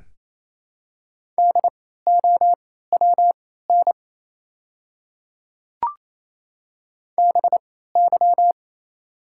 Morse code, heard as a keyed tone, about 27 wpm, spaced as for 16 wpm, 700 Hz. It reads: DOWN BY
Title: 3x Character Spacing - Top 100 Words - 28wpm